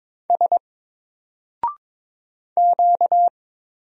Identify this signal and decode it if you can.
Morse code, heard as a keyed tone, about 22 wpm, 700 Hz. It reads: S Q
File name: Single-Letter-Number-22wpm